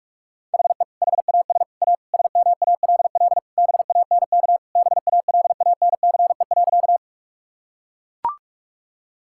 Morse code, heard as a keyed tone, about 41 wpm, 700 Hz. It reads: HE HAS A SMALL BANK BALANCE.
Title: Sentences from Top 1000 Words - Review Part 6 - 45wpm